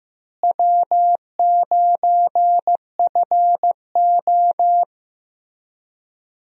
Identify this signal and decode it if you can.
Morse code, heard as a keyed tone, about 15 wpm, 700 Hz. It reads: W9FO
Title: USA Call Signs - 4 Characters - Repeated 3x - 15wpm